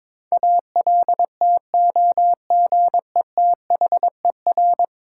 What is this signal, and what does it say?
Morse code, heard as a keyed tone, about 22 wpm, 700 Hz. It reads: ALTOGETHER
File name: practice / 22wpm / 10 Letter Words - First 3 Letters Given - 22wpm